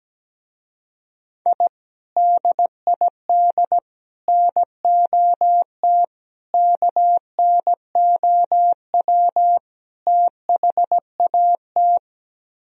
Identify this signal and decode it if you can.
Morse code, heard as a keyed tone, about 17 wpm, 700 Hz. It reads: I DID NOT KNOW THAT
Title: Speed-Racing - Sets of 5 Words 17wpm